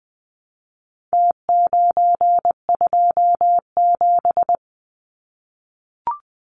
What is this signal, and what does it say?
Morse code, heard as a keyed tone, about 19 wpm, 700 Hz. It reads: T927
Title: Numbers - Contest Style - 4-Digit 20wpm